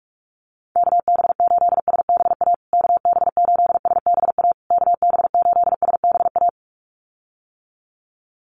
Morse code, heard as a keyed tone, about 44 wpm, 690 Hz. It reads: KB8SBA KB8SBA KB8SBA
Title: USA Call Signs Repeated 3x 45wpm